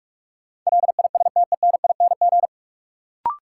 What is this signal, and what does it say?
Morse code, heard as a keyed tone, about 46 wpm, 700 Hz. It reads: LISTENING
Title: Sets of 5 Words - New Word Drill - 45wpm